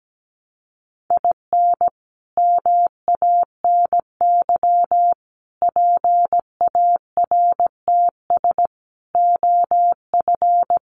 Morse code, heard as a keyed tone, about 17 wpm, 700 Hz. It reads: IN MANY PARTS OF
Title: Sets of 5 Words Encore 17wpm